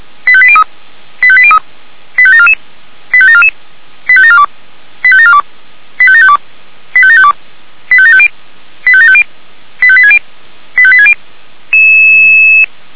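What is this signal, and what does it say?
Five-tone selective call (ZVEI), heard as quick runs of five tones, tones of about 70 ms. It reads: 85803 85803 85744 85744 85742 85742 85752 85752 85750 85750 85860 85860
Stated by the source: ZVEI five-tone selective call received by SDR.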